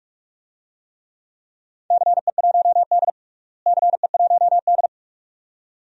Morse code, heard as a keyed tone, about 46 wpm, 700 Hz. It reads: CE1D CE1D
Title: DXCC-100 Call Signs - 4 Characters - Repeated 2x - 45wpm